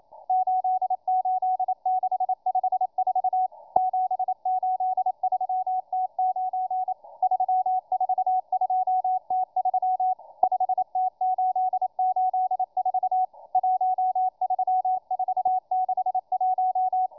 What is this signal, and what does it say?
Morse code, heard as a keyed tone, about 27 wpm, 740 Hz, with radio noise. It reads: 88654 783T9 342T3 5T884 13461